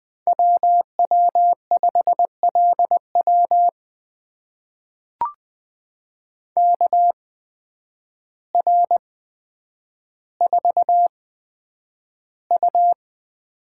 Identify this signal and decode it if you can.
Morse code, heard as a keyed tone, about 20 wpm, spaced as for 9 wpm, 700 Hz. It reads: WW5LW KR4U
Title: USA Call Signs - 8x Character Spacing - 20wpm